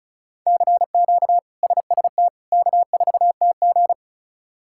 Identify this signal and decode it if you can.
Morse code, heard as a keyed tone, about 36 wpm, 700 Hz. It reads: CQ SST K4TG
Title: K1USN SST QSO Practice - 35wpm